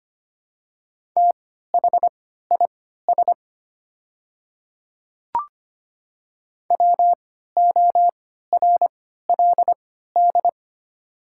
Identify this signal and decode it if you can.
Morse code, heard as a keyed tone, about 25 wpm, spaced as for 14 wpm, 700 Hz. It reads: THIS WORLD
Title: 3x Character Spacing - Top 300 Words - 25wpm